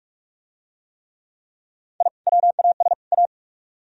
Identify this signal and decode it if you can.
Morse code, heard as a keyed tone, about 44 wpm, 700 Hz. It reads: I WAS A